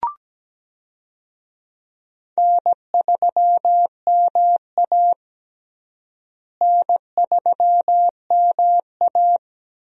Morse code, heard as a keyed tone, about 17 wpm, 700 Hz. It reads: N3MA N3MA
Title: USA Call Signs - 4 Characters - Repeated 3x - 17wpm